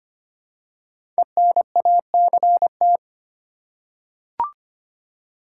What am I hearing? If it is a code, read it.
Morse code, heard as a keyed tone, about 25 wpm, 700 Hz. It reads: ENACT
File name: 5 Letter Words - First 3 Letters Given - 25wpm